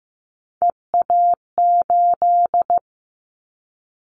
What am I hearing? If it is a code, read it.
Morse code, heard as a keyed tone, about 15 wpm, 700 Hz. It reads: EA8
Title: Callsign Prefixes for Top 50 DXCC - 15wpm